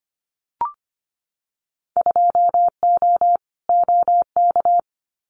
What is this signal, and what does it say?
Morse code, heard as a keyed tone, about 25 wpm, 700 Hz. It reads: 2O OK
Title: Winter Field Day Exchanges Encore 25wpm